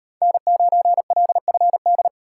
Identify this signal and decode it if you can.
Morse code, heard as a keyed tone, about 38 wpm, 700 Hz. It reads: N9LFD